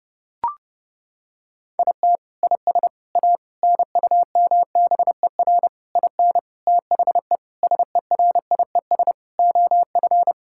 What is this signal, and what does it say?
Morse code, heard as a keyed tone, about 32 wpm, 700 Hz. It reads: IT IS A NUMBER IN THE SERIES OF
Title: Sentences from Top 1000 Words - Review Part 5 - 30wpm